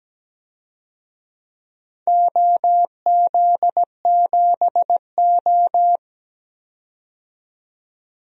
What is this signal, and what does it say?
Morse code, heard as a keyed tone, about 17 wpm, 700 Hz. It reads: OZ7O
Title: DXCC-100 Call Signs - 4 Characters - 17wpm